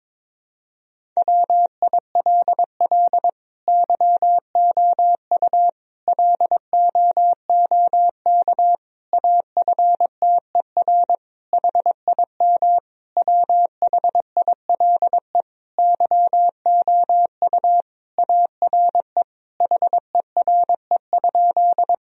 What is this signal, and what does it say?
Morse code, heard as a keyed tone, about 22 wpm, 700 Hz. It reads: WILL YOU LOOK AFTER HIM WHILE YOU ARE HERE?
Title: Sentences from Top 1000 Words - Review Part 1 - 22wpm